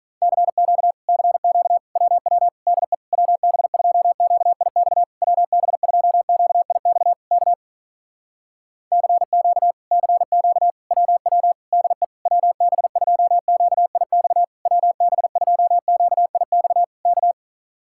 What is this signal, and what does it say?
Morse code, heard as a keyed tone, about 46 wpm, 700 Hz. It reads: CQ CQ WW DE WB2QIX WB2QIX K CQ CQ WW DE WB2QIX WB2QIX K